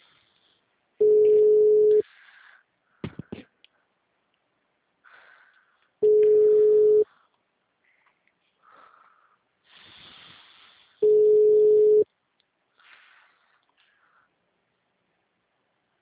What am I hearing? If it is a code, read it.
Ringback tone (Europe, 425 Hz).
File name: out- -2001- - - .2647